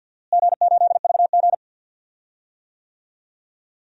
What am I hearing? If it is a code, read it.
Morse code, heard as a keyed tone, about 52 wpm, 700 Hz. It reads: G8VG
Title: DXCC-100 Call Signs - 4 Characters - Repeated 3x - 50wpm